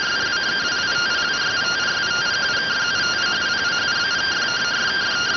Two-tone teleprinter signal, about 170 Hz shift.